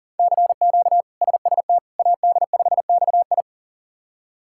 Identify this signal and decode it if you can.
Morse code, heard as a keyed tone, about 40 wpm, 700 Hz. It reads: CQ SST AD5XI
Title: K1USN SST QSO Practice - 40wpm